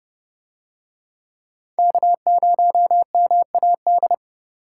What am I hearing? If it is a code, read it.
Morse code, heard as a keyed tone, about 29 wpm, 700 Hz. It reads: K0MAD